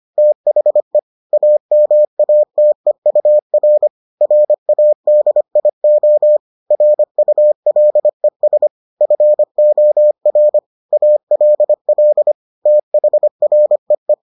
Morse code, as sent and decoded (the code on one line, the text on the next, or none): - .... / .- -- .- - . ..- .-. / .-. .- -.. .. --- / .-. ..- .-.. ... / ..-. --- .-. / .- .-.. .-.. / - .... .-. .
THE AMATEUR RADIO RULES FOR ALL THREE